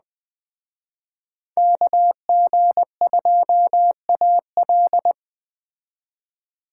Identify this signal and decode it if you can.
Morse code, heard as a keyed tone, about 19 wpm, 700 Hz. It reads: KG2AL